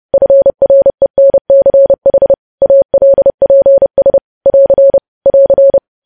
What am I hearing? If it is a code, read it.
Morse code, heard as a keyed tone, about 30 wpm, 550 Hz. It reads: FRENCH ALPS <AR> <AR>
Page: news headlines in morse code at 30 wpm